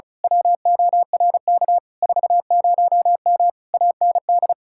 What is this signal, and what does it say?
Morse code, heard as a keyed tone, about 32 wpm, 700 Hz. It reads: WORK 40M AND
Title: Speed-Racing Common QSO Phrases Encore - 35wpm